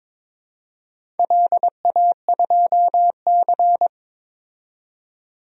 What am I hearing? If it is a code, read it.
Morse code, heard as a keyed tone, about 22 wpm, 700 Hz. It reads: LA2C
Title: DXCC-100 Call Signs - 4 Characters - 22wpm